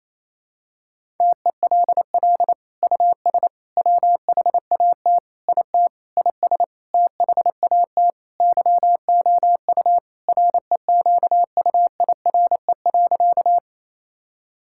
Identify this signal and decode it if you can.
Morse code, heard as a keyed tone, about 28 wpm, 700 Hz. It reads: TELL US WHAT IT IS THAT YOU REQUIRE.
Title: Sentences from Top 1000 Words - Review Part 6 - 28wpm